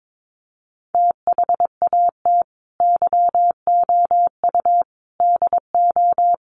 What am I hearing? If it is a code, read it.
Morse code, heard as a keyed tone, about 22 wpm, 700 Hz. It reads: THAT YOU DO
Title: Sets of 4 Words 22wpm